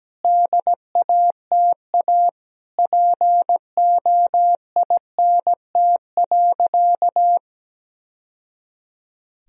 Morse code, heard as a keyed tone, about 17 wpm, 700 Hz. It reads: DATA POINT.
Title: Sentences from Top 1000 Words - Review Part 3 - 17wpm